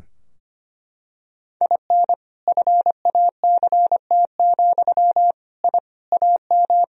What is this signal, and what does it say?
Morse code, heard as a keyed tone, about 25 wpm, 700 Hz. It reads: IN FACT, I AM
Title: Sentences from Top 1000 Words - Review Part 3 - 25wpm